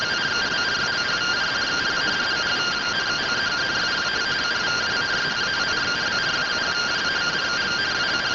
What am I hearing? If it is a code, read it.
Two-tone teleprinter signal, about 170 Hz shift.